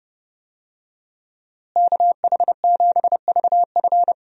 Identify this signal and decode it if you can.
Morse code, heard as a keyed tone, about 31 wpm, 700 Hz. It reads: KH7VF
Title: USA Call Signs - 5 Characters - Repeated 2x - 30wpm